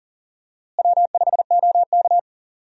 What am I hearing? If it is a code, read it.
Morse code, heard as a keyed tone, about 40 wpm, 700 Hz. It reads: W5OK
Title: USA Call Signs - 4 Characters - Repeated 2x - 40wpm